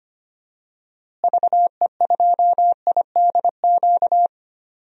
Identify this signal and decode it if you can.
Morse code, heard as a keyed tone, about 25 wpm, 700 Hz. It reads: VE2IDQ
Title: DXCC-100 Call Signs - 6 Characters - Repeated 2x - 25wpm